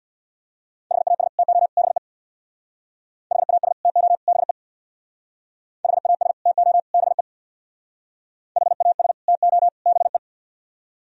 Morse code, heard as a keyed tone, about 65 wpm, 700 Hz. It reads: HAS TO BE HAS TO BE HAS TO BE HAS TO BE